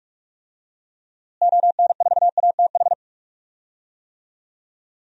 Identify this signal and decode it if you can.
Morse code, heard as a keyed tone, about 41 wpm, 700 Hz. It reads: ON4ATH
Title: DXCC-100 Call Signs - 6 Characters - Repeated 2x - 45wpm